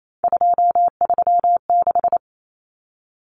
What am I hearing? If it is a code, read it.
Morse code, heard as a keyed tone, about 28 wpm, 700 Hz. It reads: 236
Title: Numbers - 3-Digit 28wpm